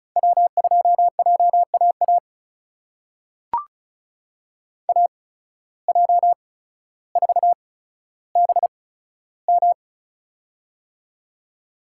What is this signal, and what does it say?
Morse code, heard as a keyed tone, about 36 wpm, spaced as for 15 wpm, 700 Hz. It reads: W2JAA AJ4BM
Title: USA Call Signs - 5 Characters - 8x Character Spacing - 35wpm